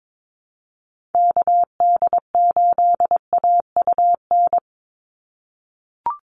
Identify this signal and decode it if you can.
Morse code, heard as a keyed tone, about 22 wpm, 700 Hz. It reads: KD8AUN
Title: USA Call Signs 22wpm